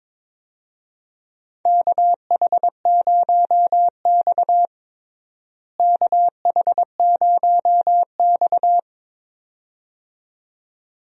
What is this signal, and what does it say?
Morse code, heard as a keyed tone, about 22 wpm, 700 Hz. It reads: KH0X KH0X
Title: USA Call Signs - 4 Characters - Repeated 2x - 22wpm